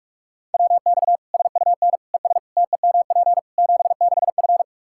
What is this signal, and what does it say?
Morse code, heard as a keyed tone, about 42 wpm, 700 Hz. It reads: WX SUN ES TEMP 76F
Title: Practice QSO - Exchange 2 - Info - 45wpm